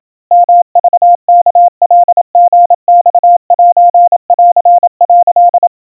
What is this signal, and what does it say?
Morse code (Japanese Wabun code), heard as a keyed tone, about 27 wpm, 700 Hz. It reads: ヨクワカリマセン」
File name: kawa / Morse sample / kamyu 100 cpm 700 Hz Crispy